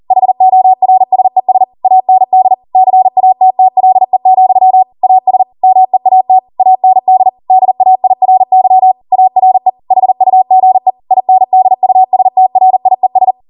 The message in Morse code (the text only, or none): HORSES AND CATTLE, AS MEAT AND DAIRY ARE HUGE INDUSTRIES